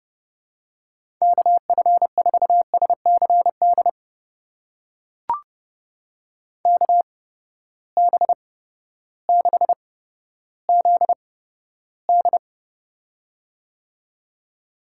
Morse code, heard as a keyed tone, about 31 wpm, spaced as for 14 wpm, 700 Hz. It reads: KF4SCD KB6ZD
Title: USA Call Signs - 8x Character Spacing - 30wpm